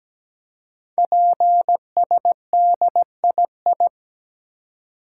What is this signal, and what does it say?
Morse code, heard as a keyed tone, about 17 wpm, 700 Hz. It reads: PSDII
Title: Code Groups - 5 Letter:Numbers - 17wpm